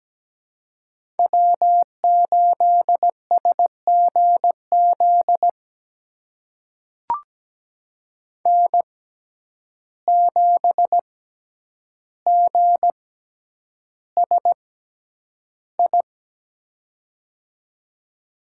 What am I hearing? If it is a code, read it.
Morse code, heard as a keyed tone, about 17 wpm, spaced as for 9 wpm, 700 Hz. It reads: W8SGZ N7GSI
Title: USA Call Signs - 5 Characters - 6x Character Spacing - 17wpm